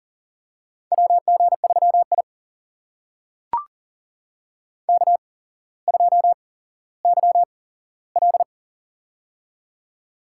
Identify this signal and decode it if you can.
Morse code, heard as a keyed tone, about 39 wpm, spaced as for 17 wpm, 700 Hz. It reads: WG3I K2YL
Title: USA Call Signs - 4 Characters - 8x Character Spacing - 40wpm